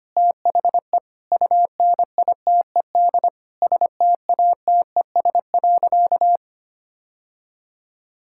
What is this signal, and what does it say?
Morse code, heard as a keyed tone, about 25 wpm, 700 Hz. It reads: THE UNITED STATES.
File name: practice / 25wpm / Sentences with 4-Word Phrases - 25wpm